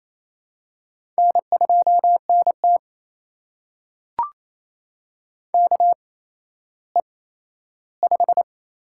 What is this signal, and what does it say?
Morse code, heard as a keyed tone, about 28 wpm, 700 Hz. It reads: N 2 N T K E 5
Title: USA Call Signs - 4 Characters - 8x Character Spacing - 28wpm